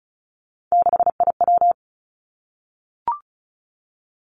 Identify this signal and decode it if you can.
Morse code, heard as a keyed tone, about 36 wpm, 700 Hz. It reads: 6IW
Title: Three Challenging Letter-Number Combinations 35wpm